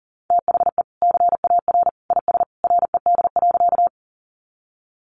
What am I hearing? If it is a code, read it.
Morse code, heard as a keyed tone, about 40 wpm, 700 Hz. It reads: THE CAR IS RED.
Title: Easy Phrases 40wpm